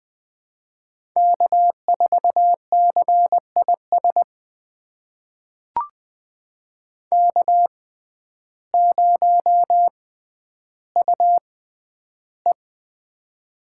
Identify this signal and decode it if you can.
Morse code, heard as a keyed tone, about 20 wpm, spaced as for 11 wpm, 700 Hz. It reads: K4CIS K0UE
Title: USA Call Signs - 6x Character Spacing - 20wpm